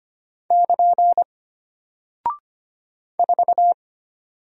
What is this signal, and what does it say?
Morse code, heard as a keyed tone, about 25 wpm, 700 Hz. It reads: <KN> 4
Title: Single Character (Letter:Number:Punctuation:Prosign) Alt 25wpm